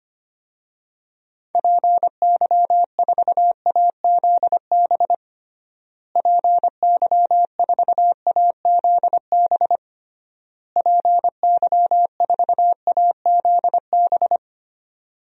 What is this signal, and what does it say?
Morse code, heard as a keyed tone, about 25 wpm, 700 Hz. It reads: PY4AZB PY4AZB PY4AZB
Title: DXCC-100 Call Signs - 6 Characters - Repeated 3x - 25wpm